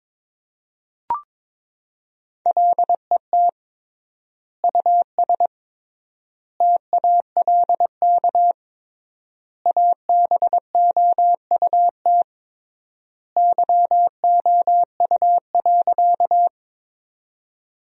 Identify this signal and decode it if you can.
Morse code, heard as a keyed tone, about 22 wpm, 700 Hz. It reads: LET US TALK ABOUT YOU.
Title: 3x Word-Spacing - Sentences from Top 400 Words - 22wpm